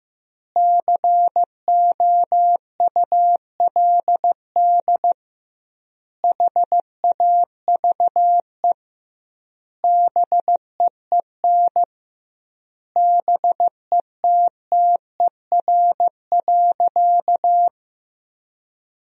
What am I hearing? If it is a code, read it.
Morse code, heard as a keyed tone, about 15 wpm, 700 Hz. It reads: COULD HAVE BEEN BETTER.
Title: 2x Word-Spacing - Sentences from Top 500 Words - 15wpm